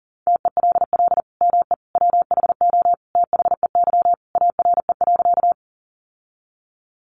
Morse code, heard as a keyed tone, about 40 wpm, 700 Hz. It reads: TELL ME WHO THEY ARE.
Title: Sentences from Top 700 Words 40wpm